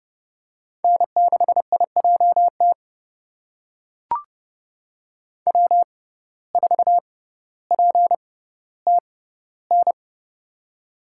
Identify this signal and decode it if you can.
Morse code, heard as a keyed tone, about 30 wpm, spaced as for 15 wpm, 690 Hz. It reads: N6IJT W4PTN
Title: USA Call Signs - 5 Characters - 6x Character Spacing - 30wpm